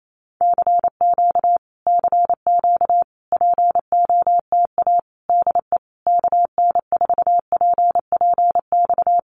Morse code, heard as a keyed tone, about 28 wpm, 700 Hz. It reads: CQ CQ POTA DE KN4PPX
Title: Common QSO Phrases Encore 28wpm